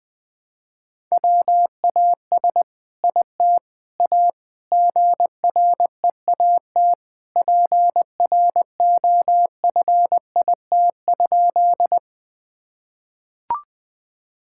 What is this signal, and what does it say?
Morse code, heard as a keyed tone, about 20 wpm, 700 Hz. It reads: WAS IT A GREAT PROFIT?
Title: Sentences from Top 1000 Words - Review Part 5 - 20wpm